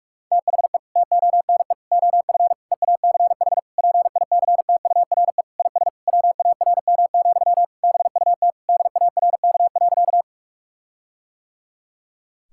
Morse code, heard as a keyed tone, about 46 wpm, 700 Hz. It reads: THE TONE OF EACH PICTURE IS WARM, BUT DARK.
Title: Sentences from Top 1600 Words 45wpm